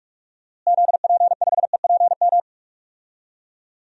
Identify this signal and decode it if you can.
Morse code, heard as a keyed tone, about 45 wpm, 700 Hz. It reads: ZP5EPM
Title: DXCC-100 Call Signs - 6 Characters - Repeated 2x - 45wpm